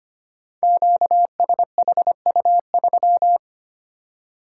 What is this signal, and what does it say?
Morse code, heard as a keyed tone, about 25 wpm, 700 Hz. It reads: QSHU3
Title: Code Groups - 5 Letter:Numbers - 25wpm